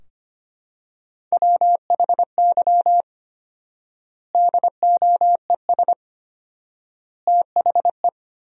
Morse code, heard as a keyed tone, about 25 wpm, 700 Hz. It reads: WHY DOES THE
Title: 4x Word-Spacing - Sentences from Top 500 Words - 25wpm